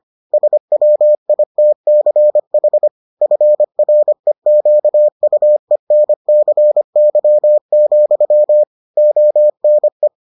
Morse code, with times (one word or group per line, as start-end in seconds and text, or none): 0.3-2.9 SWITCH
3.2-8.6 FREQUENCY,
9.0-10.1 ONE